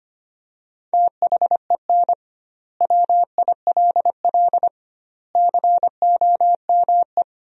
Morse code, heard as a keyed tone, about 25 wpm, 700 Hz. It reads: THEN WILL COME
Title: 2x Word-Spacing - Sentences from Top 200 Words - 25wpm